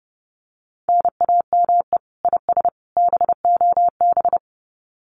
Morse code, heard as a keyed tone, about 30 wpm, 700 Hz. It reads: NAME IS BOB
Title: Common QSO Phrases 30wpm